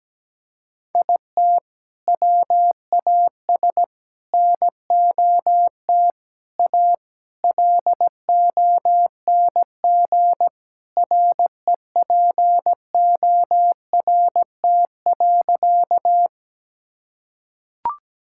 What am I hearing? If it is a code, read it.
Morse code, heard as a keyed tone, about 17 wpm, 700 Hz. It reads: IT WAS NOT A LONG REPORT.
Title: Sentences from Top 1000 Words - Review Part 6 - 17wpm